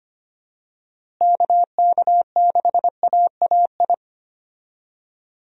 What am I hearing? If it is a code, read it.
Morse code, heard as a keyed tone, about 25 wpm, 700 Hz. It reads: KK6AAI
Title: USA Call Signs - 6 Characters - Repeated 3x - 25wpm